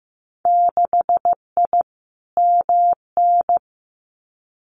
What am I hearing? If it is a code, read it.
Morse code, heard as a keyed tone, about 15 wpm, 700 Hz. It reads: 6I MN
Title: Winter Field Day Exchanges Encore 15wpm